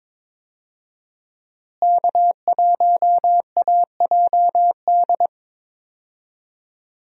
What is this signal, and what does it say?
Morse code, heard as a keyed tone, about 22 wpm, 700 Hz. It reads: K1AJD
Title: USA Call Signs - 5 Characters - Repeated 2x - 22wpm